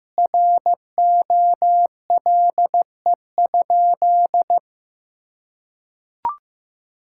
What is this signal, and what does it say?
Morse code, heard as a keyed tone, about 16 wpm, 700 Hz. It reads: ROLE?
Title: Sentences from Top 1000 Words - Review Part 6 - 15wpm